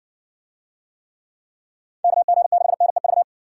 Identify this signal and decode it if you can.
Morse code, heard as a keyed tone, about 60 wpm, 700 Hz. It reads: KG6NEV